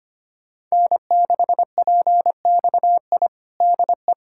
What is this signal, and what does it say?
Morse code, heard as a keyed tone, about 25 wpm, 700 Hz. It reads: N6PXI DE
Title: USA Callsign DE Callsign 25wpm